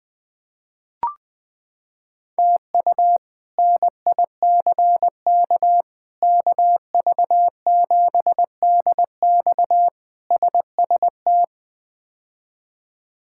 Morse code, heard as a keyed tone, about 20 wpm, 700 Hz. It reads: TU NICK KV7DX SST
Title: K1USN SST QSO Practice - 20wpm